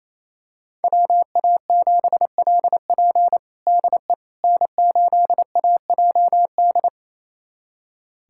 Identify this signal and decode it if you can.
Morse code, heard as a keyed tone, about 28 wpm, 700 Hz. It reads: WA7LP DE N8AJD
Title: USA Callsign DE Callsign - 5 Characters - 28wpm